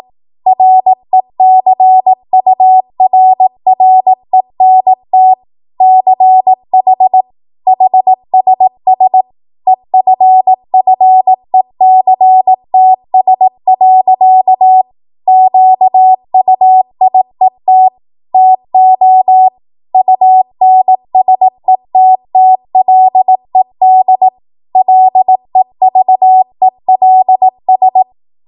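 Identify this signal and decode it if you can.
Morse code, heard as a keyed tone, about 18 wpm, 750 Hz. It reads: RECURRENT CH HSS EFFECTS. QUIET TO UNSETTLED LEVELS
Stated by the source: 18 WPM Morse code practice from a national amateur radio association's code archive.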